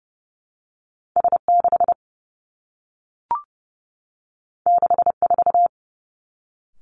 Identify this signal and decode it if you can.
Morse code, heard as a keyed tone, about 30 wpm, 700 Hz. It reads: S 6 6 4
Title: Two Challenging Letter-Number Combinations 30wpm